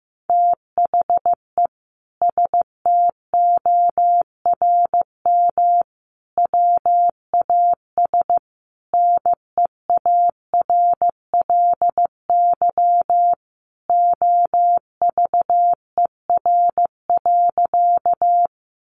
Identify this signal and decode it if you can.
Morse code, heard as a keyed tone, about 15 wpm, 700 Hz. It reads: THE STORM WAS NEARLY OVER.
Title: Sentences from Top 1500 Words 15wpm